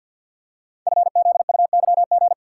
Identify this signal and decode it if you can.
Morse code, heard as a keyed tone, about 47 wpm, 700 Hz. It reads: FZUYG
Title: Code Groups - 5 Letter:Numbers - 50wpm